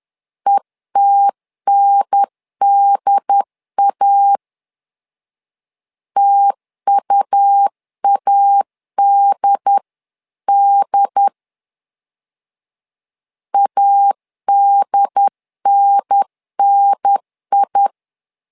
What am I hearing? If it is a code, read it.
Morse code, heard as a keyed tone, about 10 wpm, 800 Hz. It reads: ETNDA TUADD ADNNI
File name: LEZIONI / 2 UHDN / 2 UHDN 10wpm 1904